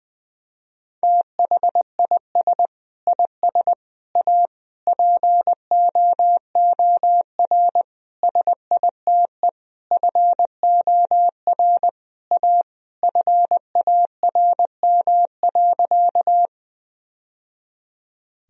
Morse code, heard as a keyed tone, about 20 wpm, 700 Hz. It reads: THIS IS A POOR SITE FOR A FARM.
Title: Sentences from Top 1000 Words - Review Part 4 - 20wpm